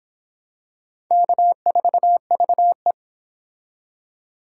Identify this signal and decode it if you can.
Morse code, heard as a keyed tone, about 25 wpm, 700 Hz. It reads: K4VE